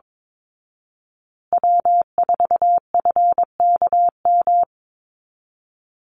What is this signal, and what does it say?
Morse code, heard as a keyed tone, about 22 wpm, 700 Hz. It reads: W4FKM
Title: Call Signs Repeated 3x - 22wpm